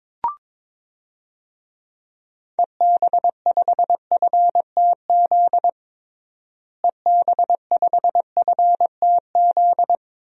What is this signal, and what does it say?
Morse code, heard as a keyed tone, about 22 wpm, 700 Hz. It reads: EB5FTZ EB5FTZ
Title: DXCC-100 Call Signs - 6 Characters - Repeated 2x - 22wpm